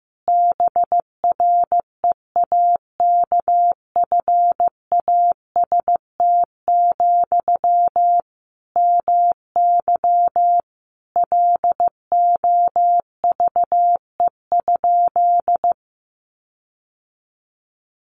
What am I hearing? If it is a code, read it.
Morse code, heard as a keyed tone, about 15 wpm, 700 Hz. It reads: BREAKFAST, MY LOVE?
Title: Sentences from Top 1500 Words 15wpm